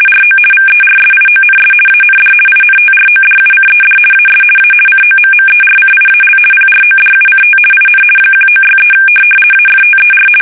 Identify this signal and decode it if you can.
RTTY, about 50 baud. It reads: RO VERSIONS AND INTRODUCED AS A COMMERCIAL PRODUCT IN 1963 AFTER
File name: fsk850-50B / teletype-corp